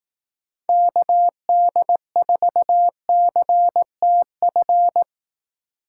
Morse code, heard as a keyed tone, about 18 wpm, 700 Hz. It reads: KD4CTF
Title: Speed Racing - USA 2x3 Call Signs 15wpm